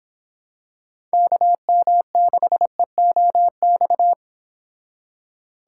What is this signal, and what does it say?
Morse code, heard as a keyed tone, about 25 wpm, 700 Hz. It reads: KM6EOX